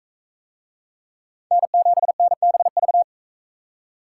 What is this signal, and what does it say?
Morse code, heard as a keyed tone, about 36 wpm, 700 Hz. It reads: N7NBV